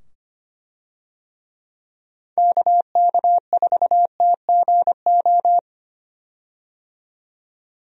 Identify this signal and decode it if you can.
Morse code, heard as a keyed tone, about 25 wpm, 700 Hz. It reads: KK4TGO
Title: USA Call Signs - 6 Characters - 25wpm